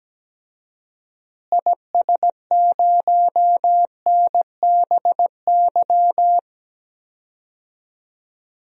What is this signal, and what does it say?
Morse code, heard as a keyed tone, about 17 wpm, 700 Hz. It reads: IS0NBY
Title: DXCC-100 Call Signs - 6 Characters - Repeated 3x - 17wpm